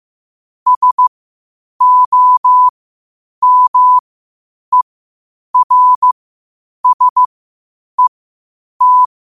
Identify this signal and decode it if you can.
Morse code, heard as a keyed tone, about 15 wpm, spaced as for 8 wpm, 1000 Hz. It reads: SOMERSET
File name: qotdfiles / 8wpm